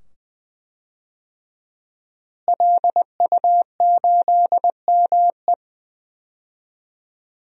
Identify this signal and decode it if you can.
Morse code, heard as a keyed tone, about 20 wpm, 700 Hz. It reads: LU8ME
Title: DXCC-100 Call Signs - 5 Characters - Repeated 2x - 20wpm